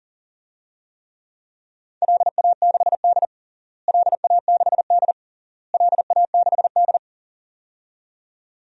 Morse code, heard as a keyed tone, about 40 wpm, 700 Hz. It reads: LA6D LA6D LA6D
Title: DXCC-100 Call Signs - 4 Characters - Repeated 3x - 40wpm